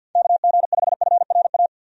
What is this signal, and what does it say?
Morse code, heard as a keyed tone, about 52 wpm, 700 Hz. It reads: KG5FRA
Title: USA Call Signs - 6 Characters - 50wpm